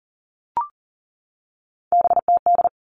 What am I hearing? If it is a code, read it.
Morse code, heard as a keyed tone, about 40 wpm, 700 Hz. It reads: BTD
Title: All Cut Numbers - 3-Digit 40wpm